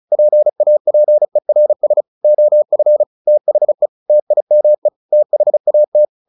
Morse code, heard as a keyed tone, about 34 wpm, 600 Hz. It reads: PAPERS OF THE TIME THAT
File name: the old man and the sea 2 35wpm